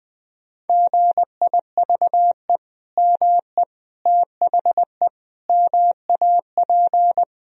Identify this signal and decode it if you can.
Morse code, heard as a keyed tone, about 20 wpm, 700 Hz. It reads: GIVE ME THE MAP
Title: Sentences from Top 1000 Words - Review Part 2 - 20wpm